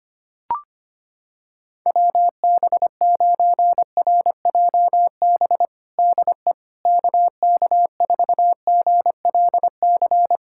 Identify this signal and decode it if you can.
Morse code, heard as a keyed tone, about 25 wpm, 700 Hz. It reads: WB9RJB DE KK4GLC
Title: USA Callsign DE Callsign 25wpm